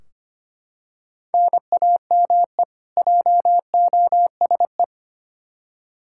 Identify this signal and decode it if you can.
Morse code, heard as a keyed tone, about 25 wpm, 700 Hz. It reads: NAME JOSE
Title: Practice QSO - Exchange 1 - Beginning - 25wpm